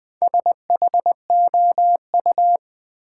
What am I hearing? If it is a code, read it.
Morse code, heard as a keyed tone, about 20 wpm, 700 Hz. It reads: SHOU
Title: Build A Word - 1 Letter At A Time - 5-Letter Words - 20wpm